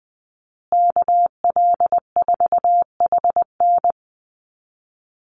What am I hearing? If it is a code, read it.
Morse code, heard as a keyed tone, about 20 wpm, 700 Hz. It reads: KL4HN
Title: USA Call Signs Repeated 3x 20wpm